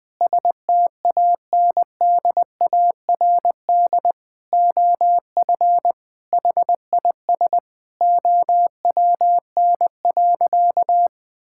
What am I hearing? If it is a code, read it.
Morse code, heard as a keyed tone, about 20 wpm, 700 Hz. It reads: STANDARD OF HIS OWN.
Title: Sentences from Top 1000 Words - Review Part 6 - 20wpm